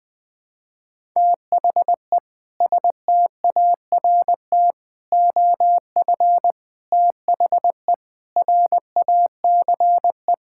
Morse code, heard as a keyed tone, about 21 wpm, 700 Hz. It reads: THE START OF THE RACE
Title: Sentences with 4-Word Phrases - 20wpm